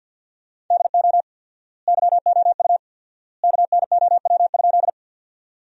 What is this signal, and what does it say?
Morse code, heard as a keyed tone, about 53 wpm, 700 Hz. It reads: DO YOU KNOW?
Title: 4x Word-Spacing - Sentences from Top 100 Words - 50wpm